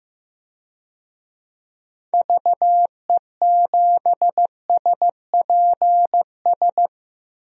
Morse code, heard as a keyed tone, about 15 wpm, 700 Hz. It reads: VE7SPS
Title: DXCC-100 Call Signs - 6 Characters - Repeated 2x - 15wpm